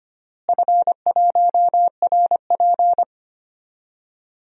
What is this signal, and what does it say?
Morse code, heard as a keyed tone, about 25 wpm, 700 Hz. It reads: F1RP
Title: DXCC-100 Call Signs - 4 Characters - Repeated 3x - 25wpm